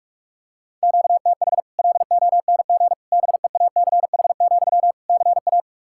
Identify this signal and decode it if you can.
Morse code, heard as a keyed tone, about 46 wpm, 700 Hz. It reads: QTH LONG BEACH, CA
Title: Practice QSO - Exchange 1 - Beginning - 45wpm